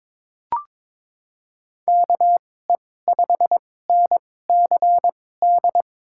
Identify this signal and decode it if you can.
Morse code, heard as a keyed tone, about 22 wpm, spaced as for 16 wpm, 700 Hz. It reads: KE5NCD
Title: USA Call Signs - 6 Characters - 2x Character Spacing - 22wpm